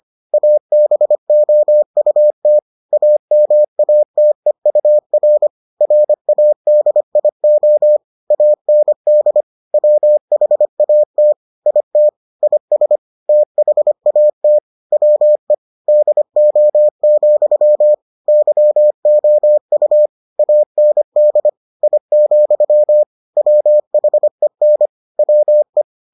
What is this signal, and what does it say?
Morse code, heard as a keyed tone, about 25 wpm, 600 Hz. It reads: ABOUT AMATEUR RADIO AND WHAT IT IS THAT WE DO, YOU AND I, WHEN WE